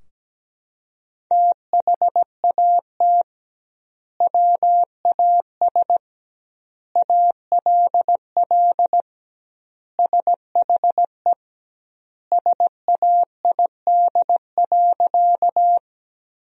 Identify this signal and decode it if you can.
Morse code, heard as a keyed tone, about 17 wpm, 700 Hz. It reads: THAT WAS ALL SHE SAID.
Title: 2x Word-Spacing - Sentences from Top 100 Words - 17wpm